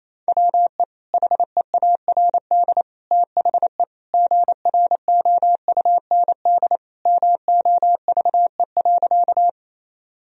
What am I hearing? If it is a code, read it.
Morse code, heard as a keyed tone, about 28 wpm, 700 Hz. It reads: WE HEARD THE GROUND MOVE.
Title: Sentences from Top 1000 Words - Review Part 2 - 28wpm